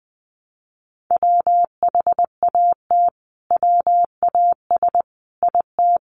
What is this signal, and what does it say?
Morse code, heard as a keyed tone, about 20 wpm, 700 Hz. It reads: WHAT WAS IT
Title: Sentences from Top 100 Words 20wpm